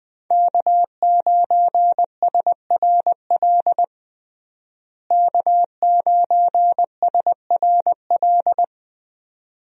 Morse code, heard as a keyed tone, about 20 wpm, 700 Hz. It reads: K9SRL K9SRL
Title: USA Call Signs - 5 Characters - Repeated 2x - 20wpm